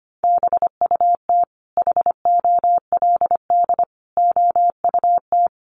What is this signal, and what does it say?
Morse code, heard as a keyed tone, about 25 wpm, 700 Hz. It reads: BUT HOLD OUT
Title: Sentences from Top 1300 Words 25wpm